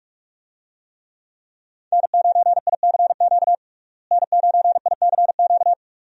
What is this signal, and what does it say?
Morse code, heard as a keyed tone, about 42 wpm, 700 Hz. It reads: N9ICQ N9ICQ
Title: USA Call Signs - 5 Characters - Repeated 2x - 45wpm